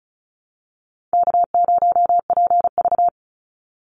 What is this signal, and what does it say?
Morse code, heard as a keyed tone, about 34 wpm, 700 Hz. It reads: K0PV
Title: Call Signs Repeated 3x - 35wpm